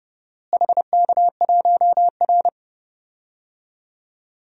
Morse code, heard as a keyed tone, about 30 wpm, 700 Hz. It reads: HK1R
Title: DXCC-100 Call Signs - 4 Characters - Repeated 3x - 30wpm